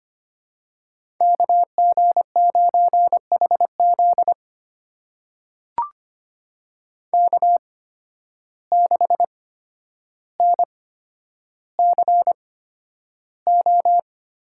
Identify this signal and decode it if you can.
Morse code, heard as a keyed tone, about 25 wpm, spaced as for 11 wpm, 700 Hz. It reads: KG9HZ K6NCO
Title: USA Call Signs - 5 Characters - 8x Character Spacing - 25wpm